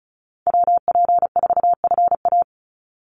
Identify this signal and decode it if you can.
Morse code, heard as a keyed tone, about 34 wpm, 700 Hz. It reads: WP4FA
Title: USA Call Signs 35wpm